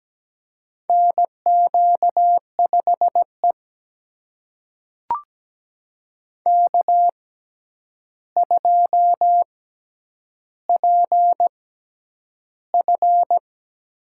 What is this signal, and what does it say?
Morse code, heard as a keyed tone, about 17 wpm, spaced as for 9 wpm, 700 Hz. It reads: NQ5E K2PF
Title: USA Call Signs - 4 Characters - 6x Character Spacing - 17wpm